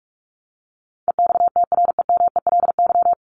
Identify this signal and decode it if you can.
Morse code, heard as a keyed tone, about 44 wpm, 700 Hz. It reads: EXTREMELY
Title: Top 751-1000 Words 45wpm